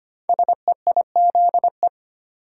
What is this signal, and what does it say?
Morse code, heard as a keyed tone, about 25 wpm, 700 Hz. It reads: SEIZE
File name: 5 Letter Words - First 3 Letters Given - 25wpm